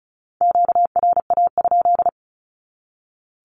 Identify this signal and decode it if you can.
Morse code, heard as a keyed tone, about 34 wpm, 700 Hz. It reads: QRA?
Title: Q-Codes 35wpm